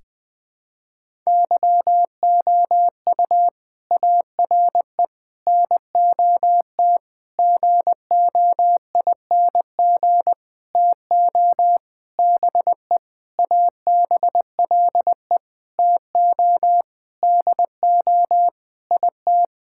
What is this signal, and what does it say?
Morse code, heard as a keyed tone, about 20 wpm, 700 Hz. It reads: YOU ARE NOT GOING TO BE ABLE TO DO IT
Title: Sentences with 7-Word Phrases - 20wpm